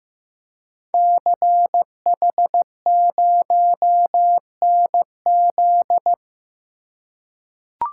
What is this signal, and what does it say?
Morse code, heard as a keyed tone, about 15 wpm, 700 Hz. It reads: CH0NZ
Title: Code Groups - 5 Letter:Numbers - 15wpm